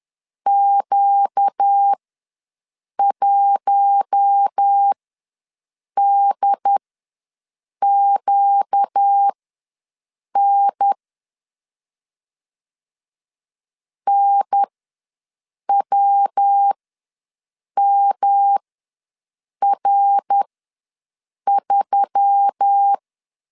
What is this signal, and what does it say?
Morse code, heard as a keyed tone, about 10 wpm, spaced as for 6 wpm, 800 Hz. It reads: Q1DQN NWMR3
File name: LEZIONI / 5 LBVC / 5 LB ws 10wpm 3616